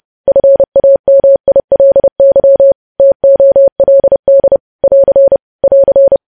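Morse code, heard as a keyed tone, about 30 wpm, 550 Hz. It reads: FAMILY TOLD <AR> <AR>
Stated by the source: Morse code news headlines practice, preceded by vvv.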